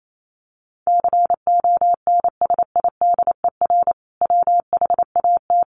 Morse code, heard as a keyed tone, about 28 wpm, 700 Hz. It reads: CONSIDER WHAT
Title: Sentences from Top 800 Words 28wpm